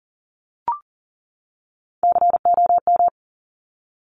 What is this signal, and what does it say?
Morse code, heard as a keyed tone, about 42 wpm, 690 Hz. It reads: COM
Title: Top Three Letter Combinations in Words 40wpm